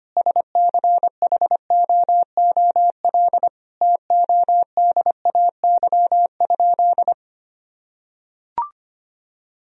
Morse code, heard as a keyed tone, about 25 wpm, 700 Hz. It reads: SCHOOL TODAY?
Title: Sentences from Top 1000 Words - Review Part 1 - 25wpm